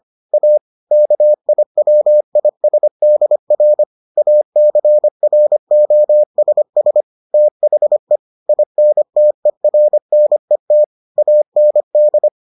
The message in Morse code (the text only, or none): A KIWISDR ACROSS THE INTERNET AND